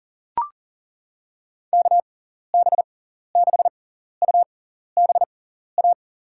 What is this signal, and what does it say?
Morse code, heard as a keyed tone, about 40 wpm, spaced as for 14 wpm, 700 Hz. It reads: KB6UBA
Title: USA Call Signs - 6 Characters - 6x Character Spacing - 40wpm